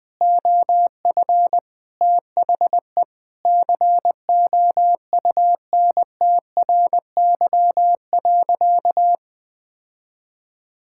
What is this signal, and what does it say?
Morse code, heard as a keyed tone, about 20 wpm, 700 Hz. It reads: OF THE COUNTRY.
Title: Sentences from Top 1000 Words - Review Part 6 - 20wpm